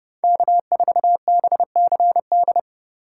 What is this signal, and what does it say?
Morse code, heard as a keyed tone, about 39 wpm, 700 Hz. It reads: K4BCD